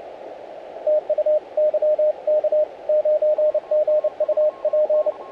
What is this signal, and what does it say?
Morse code, heard as a keyed tone, about 29 wpm, 610 Hz, with radio noise. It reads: TU YK9G UP
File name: audio mp3 files / YK9G 40m CW 2008-4-12 1-21-51